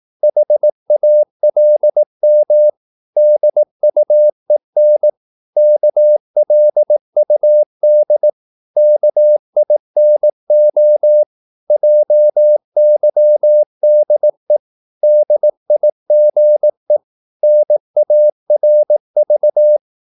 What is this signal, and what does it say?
Morse code, heard as a keyed tone, about 18 wpm, 600 Hz. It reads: HALM DUEN KLUD KINO JYDE DIGE NARV
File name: Korte ord DK 18wpm